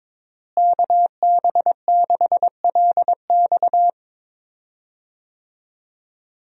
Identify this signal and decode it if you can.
Morse code, heard as a keyed tone, about 22 wpm, 700 Hz. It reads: KB6LX
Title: USA Call Signs - 5 Characters - Repeated 2x - 22wpm